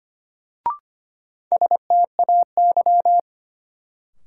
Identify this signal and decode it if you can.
Morse code, heard as a keyed tone, about 25 wpm, 700 Hz. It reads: STAY
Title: Sentences from Top 500 Words - New Word Drill - 25wpm